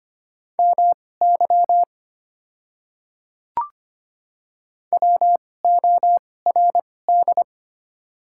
Morse code, heard as a keyed tone, about 25 wpm, spaced as for 18 wpm, 700 Hz. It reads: MY WORD
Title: 2x Character Spacing - Top 100 Words - 25wpm